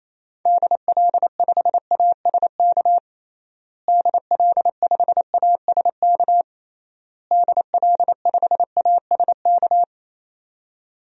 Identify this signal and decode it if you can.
Morse code, heard as a keyed tone, about 28 wpm, 700 Hz. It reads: DL5ASK DL5ASK DL5ASK
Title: DXCC-100 Call Signs - 6 Characters - Repeated 3x - 28wpm